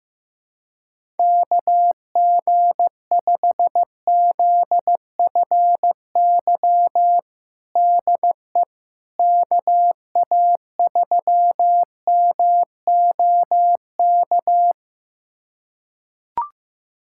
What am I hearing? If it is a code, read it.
Morse code, heard as a keyed tone, about 15 wpm, 700 Hz. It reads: KG5ZFY DE KA3MOK
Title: USA Callsign DE Callsign - 6 Characters - 15wpm